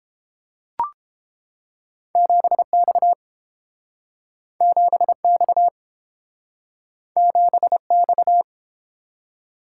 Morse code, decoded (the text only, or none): 7X 7X 7X